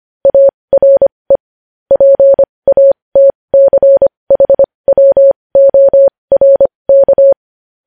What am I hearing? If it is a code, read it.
Morse code, heard as a keyed tone, about 25 wpm, spaced as for 20 wpm, 550 Hz. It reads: ARE PATCHWORK
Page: News Headlines in Morse Code at 20 WPM